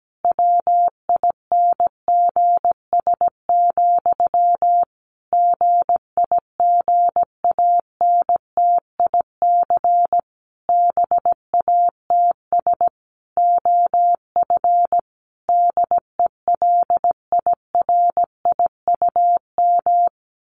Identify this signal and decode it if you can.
Morse code, heard as a keyed tone, about 17 wpm, 700 Hz. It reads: WINGS, GIGANTIC BATS OF DELIRIUM
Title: A Sound Of Thunder 17wpm